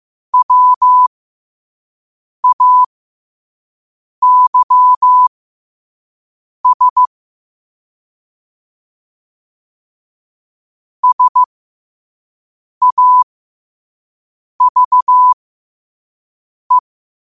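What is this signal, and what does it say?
Morse code, heard as a keyed tone, about 15 wpm, spaced as for 5 wpm, 1000 Hz. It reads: WAYS SAVE
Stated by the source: Morse code quote of the day sent at 5 wpm.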